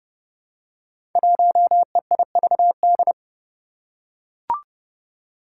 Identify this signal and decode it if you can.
Morse code, heard as a keyed tone, about 30 wpm, 700 Hz. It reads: 1EIVD
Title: Five Challenging Letter-Number Combinations 30wpm